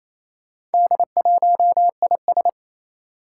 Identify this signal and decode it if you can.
Morse code, heard as a keyed tone, about 28 wpm, 700 Hz. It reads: D1IS
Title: Four Challenging Letter-Number Combinations 28wpm